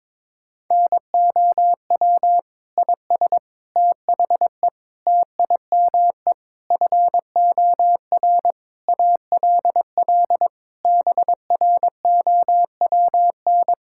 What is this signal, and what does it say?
Morse code, heard as a keyed tone, about 22 wpm, 700 Hz. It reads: NOW IS THE TIME FOR ALL BROWN
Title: Pangram Challenge v2 - 22wpm